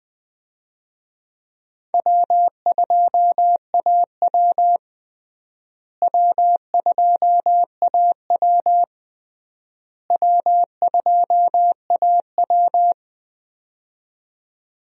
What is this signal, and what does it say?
Morse code, heard as a keyed tone, about 21 wpm, 700 Hz. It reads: W2AW W2AW W2AW
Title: USA Call Signs - 4 Characters - Repeated 3x - 20wpm